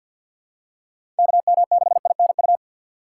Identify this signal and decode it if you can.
Morse code, heard as a keyed tone, about 52 wpm, 700 Hz. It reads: KM6INU
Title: USA Call Signs - 6x Character Spacing - 50wpm